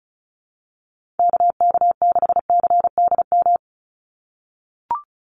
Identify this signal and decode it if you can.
Morse code, heard as a keyed tone, about 34 wpm, 700 Hz. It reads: KK6CDM
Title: USA Call Signs 35wpm